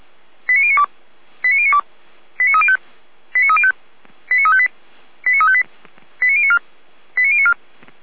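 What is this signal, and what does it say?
Five-tone selective call (ZVEI), heard as quick runs of five tones, tones of about 70 ms. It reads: 89092 89092 89305 89305 89358 89358 89094 89094
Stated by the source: Received by SDR.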